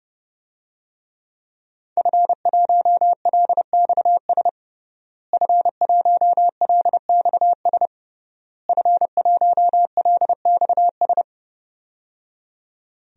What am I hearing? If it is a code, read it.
Morse code, heard as a keyed tone, about 30 wpm, 700 Hz. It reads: F1LXS F1LXS F1LXS
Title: DXCC-100 Call Signs - 5 Characters - Repeated 3x - 30wpm